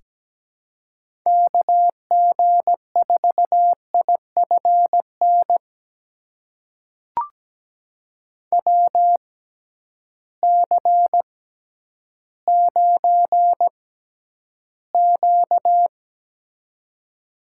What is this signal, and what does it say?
Morse code, heard as a keyed tone, about 17 wpm, spaced as for 10 wpm, 700 Hz. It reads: KG4IFN WC9Q
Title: USA Call Signs - 6x Character Spacing - 17wpm